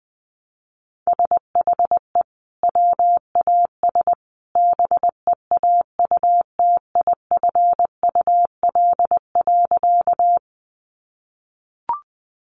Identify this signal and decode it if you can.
Morse code, heard as a keyed tone, about 20 wpm, 700 Hz. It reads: SHE WAS BEAUTIFUL.
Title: Sentences from Top 900 Words 20wpm